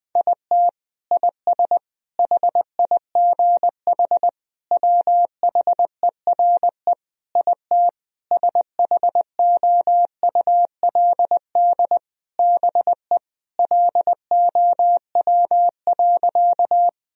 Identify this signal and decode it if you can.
Morse code, heard as a keyed tone, about 20 wpm, 700 Hz. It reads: IT IS HIGH WHERE IT SHOULD BE LOW.
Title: Sentences from Top 1000 Words - Review Part 1 - 20wpm